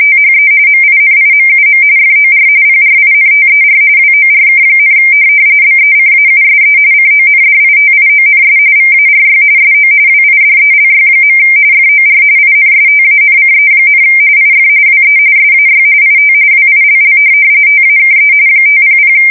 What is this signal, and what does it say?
RTTY, about 45.45 baud. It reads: THE MINIMUM FOR PRINTER WORK, BECAUSE CONDITIONS VARY SO WIDELY. FOR INSTANCE, IF WE SAY IT WILL TAKE 500
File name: fsk170 / printing-telegraph